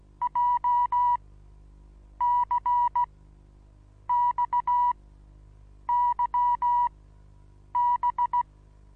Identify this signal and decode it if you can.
Morse code, heard as a keyed tone, about 17 wpm, spaced as for 7 wpm, 980 Hz. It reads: JCXYB
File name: Onlinelehrgang / morsen / DJ4UF Morsen25